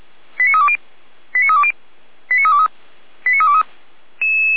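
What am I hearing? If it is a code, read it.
Five-tone selective call (ZVEI), heard as quick runs of five tones, tones of about 80 ms. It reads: 89320 89320 89323 89323
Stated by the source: ZVEI five-tone selective call 89304 received with an SDR.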